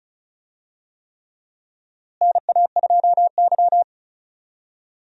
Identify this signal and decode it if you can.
Morse code, heard as a keyed tone, about 34 wpm, 700 Hz. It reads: NA2Y
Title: USA Call Signs - 4 Characters - Repeated 3x - 35wpm